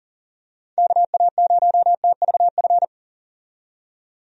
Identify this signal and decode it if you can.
Morse code, heard as a keyed tone, about 40 wpm, 700 Hz. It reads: KA0TVF
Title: USA Call Signs - 6 Characters - 40wpm